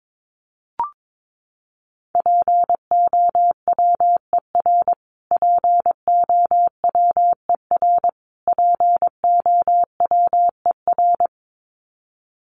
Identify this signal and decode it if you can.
Morse code, heard as a keyed tone, about 22 wpm, 700 Hz. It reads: POWER POWER POWER
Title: Top 500 Words Repeated 22wpm